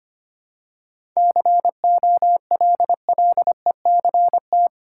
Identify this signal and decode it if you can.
Morse code, heard as a keyed tone, about 25 wpm, 700 Hz. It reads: COLLECT
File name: Easy Sentences with Hints - 25wpm